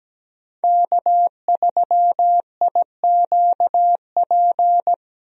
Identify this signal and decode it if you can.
Morse code, heard as a keyed tone, about 17 wpm, 700 Hz. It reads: K3IQP
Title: USA Call Signs - 5 Characters - Repeated 3x - 17wpm